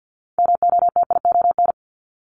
Morse code, heard as a keyed tone, about 50 wpm, 700 Hz. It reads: MOTION
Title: Multi-letter Phonemes 50wpm